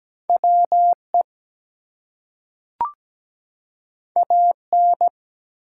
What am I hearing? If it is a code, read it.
Morse code, heard as a keyed tone, about 17 wpm, 700 Hz. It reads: WE AN
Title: Two Letter Words - 17wpm